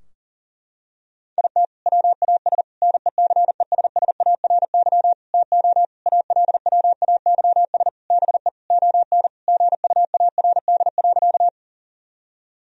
Morse code, heard as a keyed tone, about 40 wpm, 700 Hz. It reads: IT WAS NECESSARY TO ALWAYS BE ON GUARD.
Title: Sentences from Top 1000 Words - Review Part 4 - 40wpm